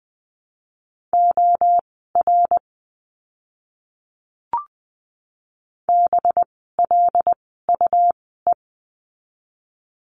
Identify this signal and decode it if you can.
Morse code, heard as a keyed tone, about 20 wpm, spaced as for 15 wpm, 700 Hz. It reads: OR BLUE
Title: 185 Lesson 30f - Letter b - Words 20wpm